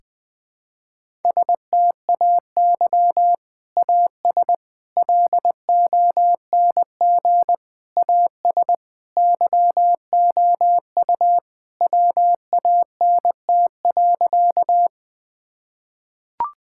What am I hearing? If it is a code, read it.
Morse code, heard as a keyed tone, about 21 wpm, 700 Hz. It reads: STAY AS LONG AS YOU WANT.
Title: Sentences with 3-Word Phrases - Spoken Beginning - 20wpm